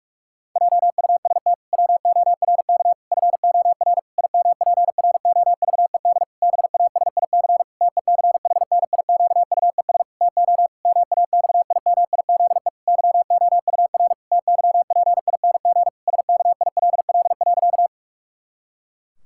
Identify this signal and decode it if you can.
Morse code, heard as a keyed tone, about 46 wpm, 700 Hz. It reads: JUST WORK FOR IMPROVED BASIC TECHNIQUES TO MAXIMIZE YOUR TYPING SKILL.
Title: Pangram Challenge v2 - 45wpm